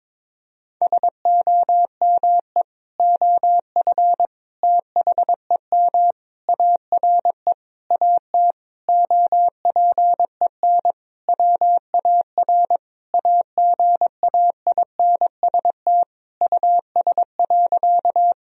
Morse code, heard as a keyed tone, about 22 wpm, 700 Hz. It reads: SOME OF THEM ARE AT OPEN WAR AGAINST US.
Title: Sentences from Top 1000 Words - Review Part 2 - 22wpm